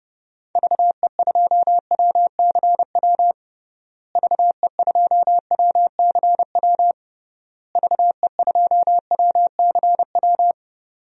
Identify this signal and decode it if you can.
Morse code, heard as a keyed tone, about 30 wpm, 700 Hz. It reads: VE2WCW VE2WCW VE2WCW
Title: DXCC-100 Call Signs - 6 Characters - Repeated 3x - 30wpm